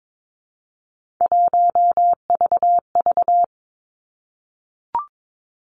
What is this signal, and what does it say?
Morse code, heard as a keyed tone, about 22 wpm, 700 Hz. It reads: 1VV
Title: Three Challenging Letter-Number Combinations 22wpm